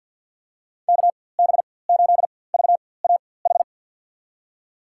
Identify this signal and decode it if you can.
Morse code, heard as a keyed tone, about 46 wpm, spaced as for 23 wpm, 700 Hz. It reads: KB8VAH
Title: USA Call Signs - 6 Characters - 4x Character Spacing - 50wpm